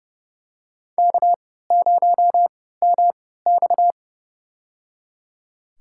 Morse code, heard as a keyed tone, about 30 wpm, spaced as for 17 wpm, 700 Hz. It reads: K0MX
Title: USA Call Signs - 3x Character Spacing - 30wpm